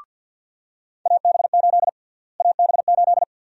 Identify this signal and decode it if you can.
Morse code, heard as a keyed tone, about 50 wpm, 700 Hz. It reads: AB8 AB8
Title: USA Call Sign Prefixes - Repeated 2x - 50wpm